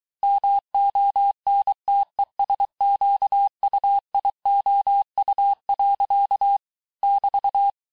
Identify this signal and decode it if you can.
Morse code, heard as a keyed tone, about 23 wpm, 800 Hz. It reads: MONTESQUIOU. <BT>
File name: cours-du-lundi files / cours-1594--teste-clair-vitesse-17mpm-puis-22mpm